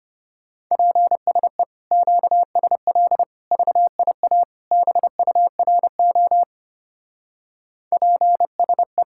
Morse code, heard as a keyed tone, about 29 wpm, 700 Hz. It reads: PSE QSL VIA BURO PSE
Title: Speed-Racing Common QSO Phrases Encore - 25wpm